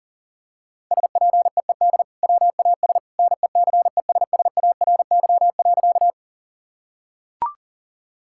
Morse code, heard as a keyed tone, about 40 wpm, 700 Hz. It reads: SPEED WAS NECESSARY.
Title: Sentences from Top 1000 Words - Review Part 5 - 40wpm